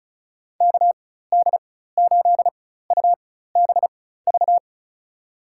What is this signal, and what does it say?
Morse code, heard as a keyed tone, about 34 wpm, spaced as for 16 wpm, 700 Hz. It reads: KD8UBV
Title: USA Call Signs - 6 Characters - 4x Character Spacing - 35wpm